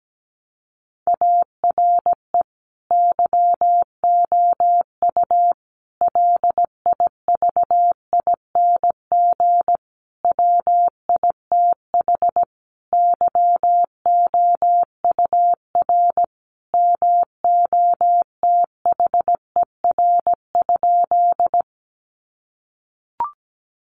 Morse code, heard as a keyed tone, about 17 wpm, 700 Hz. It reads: ARE YOU LIVING WITH YOUR MOTHER?
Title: Sentences from Top 1100 Words 17wpm